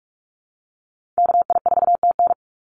Morse code, heard as a keyed tone, about 44 wpm, 700 Hz. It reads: KI4TN
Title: NA SOTA Activator Callsigns 45wpm